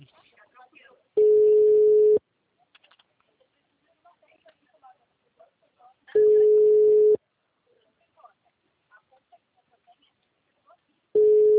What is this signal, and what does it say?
Ringback tone (Europe, 425 Hz), with speech.
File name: out- -2016- - - .8909